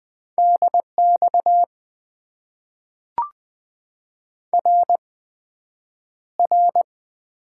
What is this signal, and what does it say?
Morse code, heard as a keyed tone, about 20 wpm, 700 Hz. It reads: DX R R
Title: Top 90 QSO Elements - 8x Character Spacing - 20wpm